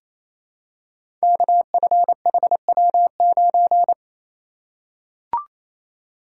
Morse code, heard as a keyed tone, about 27 wpm, 700 Hz. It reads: KFHW9
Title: Code Groups - 5 Letter:Numbers - 28wpm